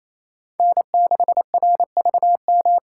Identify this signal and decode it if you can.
Morse code, heard as a keyed tone, about 27 wpm, 700 Hz. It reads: N6RVM
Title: USA Call Signs - 2x Character Spacing - 28wpm